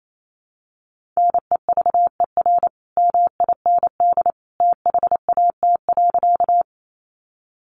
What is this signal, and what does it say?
Morse code, heard as a keyed tone, about 28 wpm, 700 Hz. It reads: NEVER MIND THAT.
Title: Sentences from Top 500 Words 28wpm